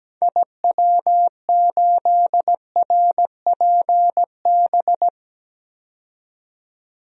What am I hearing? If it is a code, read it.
Morse code, heard as a keyed tone, about 17 wpm, 700 Hz. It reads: IW8RPB
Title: DXCC-100 Call Signs - 6 Characters - 17wpm